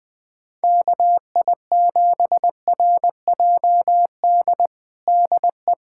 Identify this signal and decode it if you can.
Morse code, heard as a keyed tone, about 20 wpm, 700 Hz. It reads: KI7RJD DE
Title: USA Callsign DE Callsign - 6 Characters - 20wpm